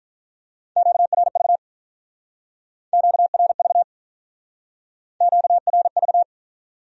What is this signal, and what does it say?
Morse code, heard as a keyed tone, about 43 wpm, 700 Hz. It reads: QRV QRV QRV
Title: Speed-Racing Q-Codes - 35wpm